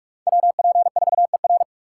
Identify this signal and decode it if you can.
Morse code, heard as a keyed tone, about 65 wpm, spaced as for 55 wpm, 700 Hz. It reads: WP4ER
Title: Speed Racing - USA 2x2 Call Signs 45wpm